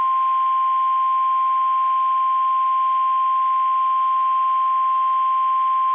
Emergency Alert System (SAME) tones (the 1050 Hz attention tone). It is heard from 0.9 to 5.6 s.
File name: eas / KBBX / KBBX